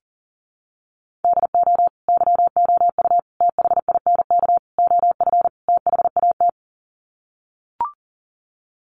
Morse code, heard as a keyed tone, about 40 wpm, 700 Hz. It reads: DO YOU THINK OF THAT
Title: Sets of 5 Words Encore2 40wpm